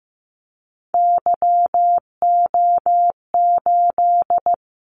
Morse code, heard as a keyed tone, about 15 wpm, 700 Hz. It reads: YO8
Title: Callsign Prefixes for Top 25 DXCC - 15wpm